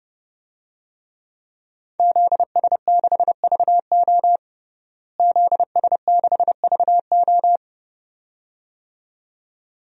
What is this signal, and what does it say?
Morse code, heard as a keyed tone, about 29 wpm, 700 Hz. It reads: ZS6VO ZS6VO
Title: DXCC-100 Call Signs - 5 Characters - Repeated 2x - 30wpm